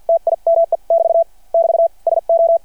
Morse code, heard as a keyed tone, about 53 wpm, spaced as for 45 wpm, 660 Hz. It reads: TIME <BT> <BT> SO
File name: archived / CRSnet